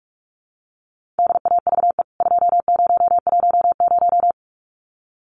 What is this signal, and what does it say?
Morse code, heard as a keyed tone, about 44 wpm, 700 Hz. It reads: DAVE 2010
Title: CWT Exchange 45wpm